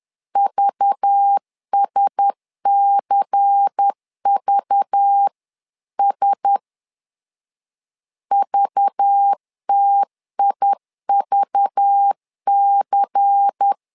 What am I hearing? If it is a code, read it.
Morse code, heard as a keyed tone, about 10 wpm, 800 Hz. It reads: VSCVS VTIVC
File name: LEZIONI / 5 LBVC / 5 LBVC 10wpm 2349